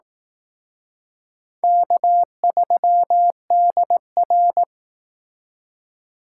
Morse code, heard as a keyed tone, about 18 wpm, 700 Hz. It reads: K3DR
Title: Speed Racing - USA 1x2 Call Signs 15wpm